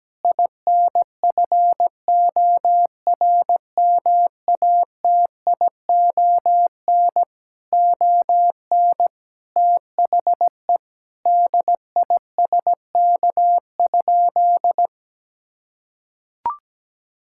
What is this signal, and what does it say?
Morse code, heard as a keyed tone, about 17 wpm, 700 Hz. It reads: INFORMATION ON THE DISK?
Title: Sentences from Top 1000 Words - Review Part 6 - 17wpm